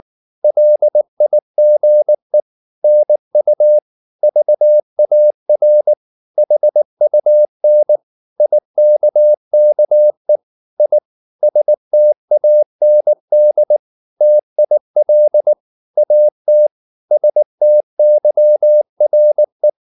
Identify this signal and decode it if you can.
Morse code, heard as a keyed tone, about 19 wpm, 600 Hz. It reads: LIGE NU VAR HUN IKKE I STAND TIL AT STYRE
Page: Ferie 19 wpm